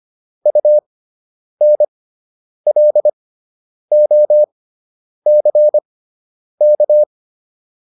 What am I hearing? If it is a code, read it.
Morse code, heard as a keyed tone, about 25 wpm, spaced as for 9 wpm, 600 Hz. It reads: UNLOCK